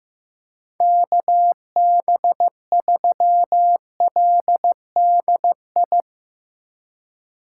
Morse code, heard as a keyed tone, about 15 wpm, 700 Hz. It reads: KB3LDI
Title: USA Call Signs - 4x Character Spacing - 15wpm